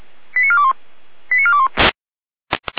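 Five-tone selective call (ZVEI), heard as quick runs of five tones, tones of about 70 ms. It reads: 89421 89421
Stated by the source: ZVEI five-tone selective call received on an SDR for Feuerwehr Orlamünde.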